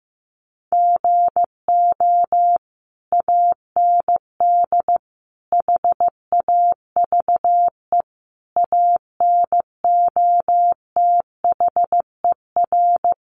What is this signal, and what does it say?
Morse code, heard as a keyed tone, about 15 wpm, 700 Hz. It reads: GO AND HAVE ANOTHER
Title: Sentences from Top 600 Words 15wpm